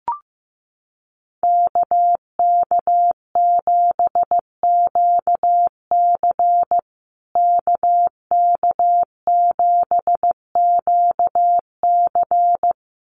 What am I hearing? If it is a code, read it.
Morse code, heard as a keyed tone, about 15 wpm, 700 Hz. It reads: KK7QC KK7QC
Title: USA Call Signs Repeated 2x 15wpm